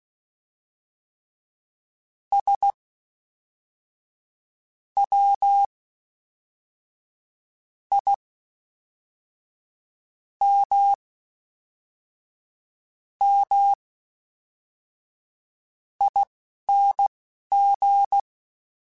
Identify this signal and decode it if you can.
Morse code, heard as a keyed tone, about 16 wpm, spaced as for 4 wpm, 800 Hz. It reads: SWIMMING
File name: Special INGs / Special INGs-mp3s / Fast ing @10